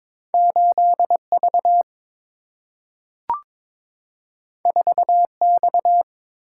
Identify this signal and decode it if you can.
Morse code, heard as a keyed tone, about 22 wpm, 700 Hz. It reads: 8 V 4 X
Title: Single-Letter Single-Number 22wpm